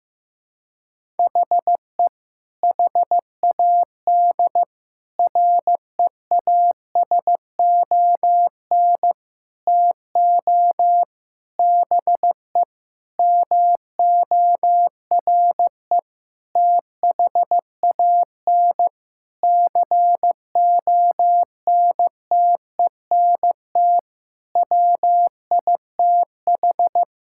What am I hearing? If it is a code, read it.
Morse code, heard as a keyed tone, about 15 wpm, 700 Hz. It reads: HE HAD REASON TO BE MORE THAN CONTENT WITH
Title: Sentences from Top 1000 Words - Review Part 6 - 15wpm